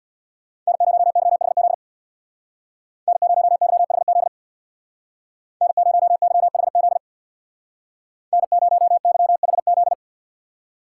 Morse code, heard as a keyed tone, about 61 wpm, 700 Hz. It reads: N0YHZ N0YHZ N0YHZ N0YHZ
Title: Speed Racing - USA 1x3 Call Signs 50wpm